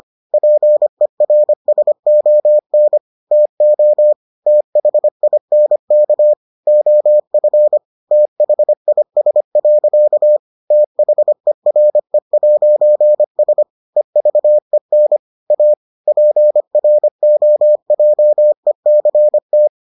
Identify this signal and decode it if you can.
Morse code, heard as a keyed tone, about 25 wpm, 600 Hz. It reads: PERSON TO THINK OF THIS. THERE'S EVEN A PROJECT